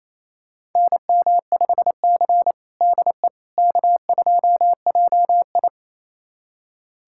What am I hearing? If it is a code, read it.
Morse code, heard as a keyed tone, about 28 wpm, 700 Hz. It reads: NM5C DE K2JI
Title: USA Callsign DE Callsign - 4 Characters - 28wpm